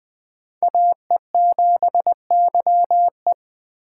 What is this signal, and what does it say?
Morse code, heard as a keyed tone, about 20 wpm, 700 Hz. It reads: AE7YE